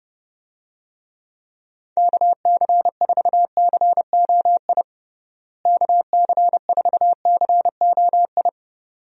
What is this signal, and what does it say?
Morse code, heard as a keyed tone, about 30 wpm, 700 Hz. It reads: KC4COI KC4COI
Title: USA Call Signs - 6 Characters - Repeated 2x - 30wpm